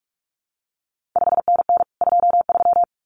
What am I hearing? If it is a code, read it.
Morse code, heard as a keyed tone, about 46 wpm, 700 Hz. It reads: 5NN 23
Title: CQ WW DX Contest Exchange Encore - 45wpm